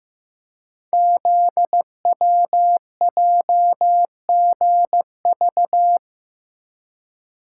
Morse code, heard as a keyed tone, about 15 wpm, 700 Hz. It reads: ZWJGV
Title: Code Groups - 5 Letters - 15wpm